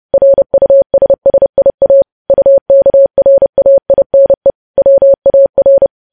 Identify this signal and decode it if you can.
Morse code, heard as a keyed tone, about 30 wpm, 550 Hz. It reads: RUSSIA UKRAINE WAR
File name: podcast / News headlines-1x-30x30